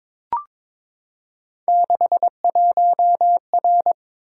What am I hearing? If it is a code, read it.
Morse code, heard as a keyed tone, about 22 wpm, 700 Hz. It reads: 61R
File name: practice / 22wpm / Three Letter-Number Combinations - 22wpm